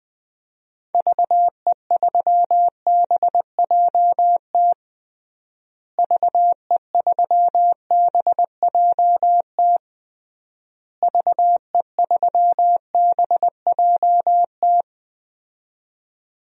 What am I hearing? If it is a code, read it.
Morse code, heard as a keyed tone, about 20 wpm, 700 Hz. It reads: VE3BJT VE3BJT VE3BJT
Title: DXCC-100 Call Signs - 6 Characters - Repeated 3x - 20wpm